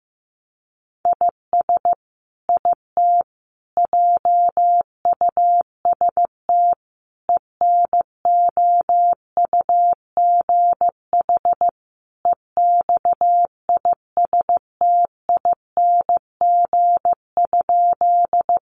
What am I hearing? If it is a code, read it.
Morse code, heard as a keyed tone, about 15 wpm, 700 Hz. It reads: IS IT JUST ENOUGH EXISTING?
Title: Sentences from Top 1500 Words 15wpm